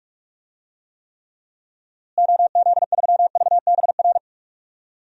Morse code, heard as a keyed tone, about 45 wpm, 700 Hz. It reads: OZ3VBR
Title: DXCC-100 Call Signs - 6 Characters - Repeated 2x - 45wpm